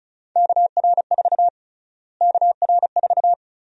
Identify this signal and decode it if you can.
Morse code, heard as a keyed tone, about 34 wpm, 700 Hz. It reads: KR4 KR4
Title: USA Call Sign Prefixes - Repeated 2x - 35wpm